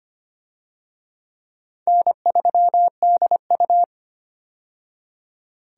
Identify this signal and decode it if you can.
Morse code, heard as a keyed tone, about 25 wpm, 700 Hz. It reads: N3DU
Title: USA Call Signs - 4 Characters - 25wpm